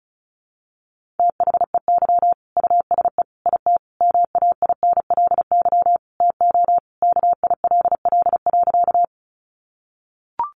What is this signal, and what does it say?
Morse code, heard as a keyed tone, about 34 wpm, 700 Hz. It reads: THEY USE IT MAINLY TO KILL.
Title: Sentences from Top 1400 Words 35wpm